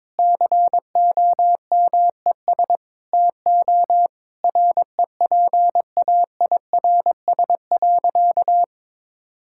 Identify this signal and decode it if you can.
Morse code, heard as a keyed tone, about 22 wpm, 700 Hz. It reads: COMES TO REPAIRS.
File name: Difficult Sentences with Hints - 22wpm